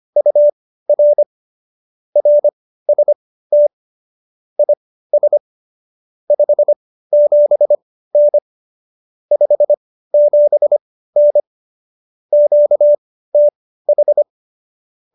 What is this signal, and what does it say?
Morse code, heard as a keyed tone, about 25 wpm, spaced as for 15 wpm, 590 Hz. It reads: UR RST IS 57N 57N QTH
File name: qso219b 15